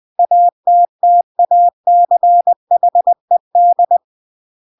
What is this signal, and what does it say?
Morse code, heard as a keyed tone, about 20 wpm, 700 Hz. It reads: ATTACHED
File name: CW / jodorowsky / 82 commandments 20-20